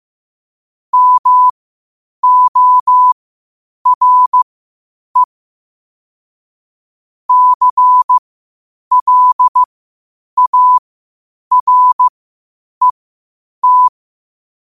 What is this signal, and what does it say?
Morse code, heard as a keyed tone, about 15 wpm, spaced as for 8 wpm, 1000 Hz. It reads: MORE CLARET
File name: qotdfiles / 8wpm